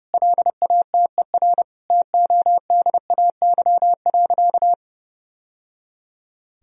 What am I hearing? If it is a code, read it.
Morse code, heard as a keyed tone, about 30 wpm, 700 Hz. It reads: LATER TODAY.
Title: Sentences from Top 1000 Words - Review Part 4 - 30wpm